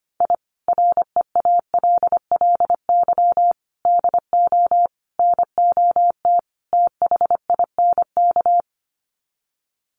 Morse code, heard as a keyed tone, about 25 wpm, 700 Hz. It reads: I REALLY DO NOT THINK
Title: Sets of 5 Words Encore2 25wpm